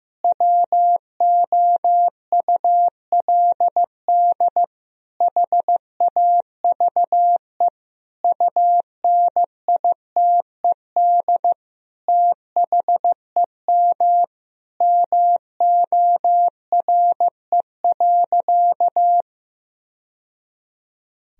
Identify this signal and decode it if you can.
Morse code, heard as a keyed tone, about 15 wpm, 700 Hz. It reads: WOULD HAVE UNITED THEM MORE.
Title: Sentences from Top 1000 Words - Review Part 4 - 15wpm